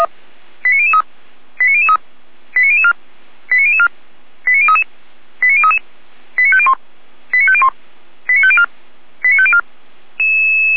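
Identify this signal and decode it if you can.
Five-tone selective call (ZVEI), heard as quick runs of five tones, tones of about 70 ms. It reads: 89003 89003 89004 89004 89033 89033 89501 89501 89504 89504
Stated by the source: ZVEI five-tone call received by SDR, fire brigade Feuerwehr Orlamünde.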